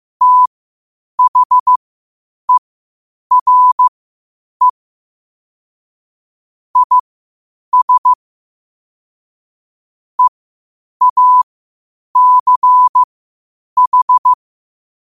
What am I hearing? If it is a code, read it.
Morse code, heard as a keyed tone, about 15 wpm, spaced as for 8 wpm, 1000 Hz. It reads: THERE IS EACH
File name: qotdfiles / 8wpm